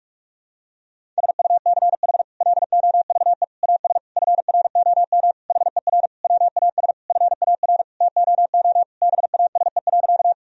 Morse code, heard as a keyed tone, about 46 wpm, 700 Hz. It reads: SUCH LOVE AS FROM HER WAS FAR TOO BASE.
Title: Sentences from Top 1000 Words - Review Part 1 - 45wpm